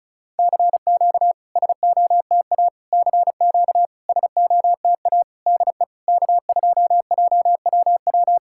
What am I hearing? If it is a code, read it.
Morse code, heard as a keyed tone, about 36 wpm, 700 Hz. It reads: CQ SOTA CQ SOTA DE K2JWW
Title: SOTA QSO Practice 35wpm